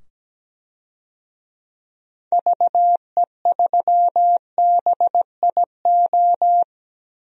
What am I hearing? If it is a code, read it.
Morse code, heard as a keyed tone, about 17 wpm, 700 Hz. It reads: VE3BIO
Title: DXCC-100 Call Signs - 6 Characters - 17wpm